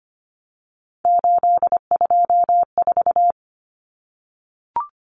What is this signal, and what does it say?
Morse code, heard as a keyed tone, about 25 wpm, 700 Hz. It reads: 824
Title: Common Cut Numbers - 3-Digit 25wpm